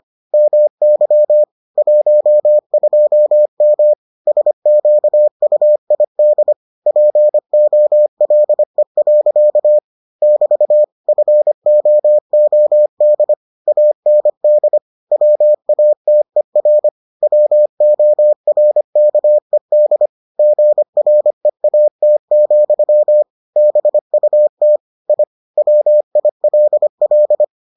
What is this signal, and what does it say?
Morse code, heard as a keyed tone, about 25 wpm, 600 Hz. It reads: MY 12M SQUID POLE. <BT> FOOD AND WATER WORKED GREAT, BUT I WILL